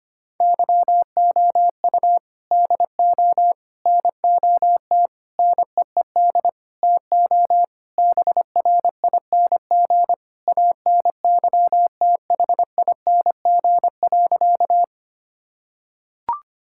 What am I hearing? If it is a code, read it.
Morse code, heard as a keyed tone, about 25 wpm, 700 Hz. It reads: YOU DO NOT NEED TO BRING ANYTHING.
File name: practice / 25wpm / Sentences with 4-Word Phrases - 25wpm